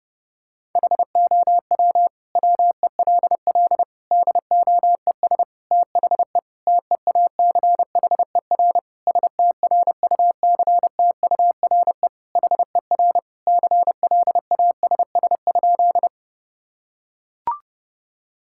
Morse code, heard as a keyed tone, about 32 wpm, 700 Hz. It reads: HOW WELL DOES THE TEACHER STRUCTURE HER CLASS?
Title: Sentences from Top 1000 Words - Review Part 6 - 30wpm